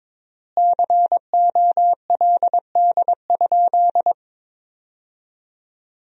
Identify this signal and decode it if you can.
Morse code, heard as a keyed tone, about 22 wpm, 700 Hz. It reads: COLD?
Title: 3x Word-Spacing - Sentences from Top 400 Words - 22wpm